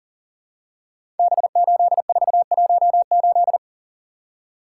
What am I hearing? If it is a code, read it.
Morse code, heard as a keyed tone, about 38 wpm, 690 Hz. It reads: B8418
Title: Code Groups - 5 Letter:Numbers - 40wpm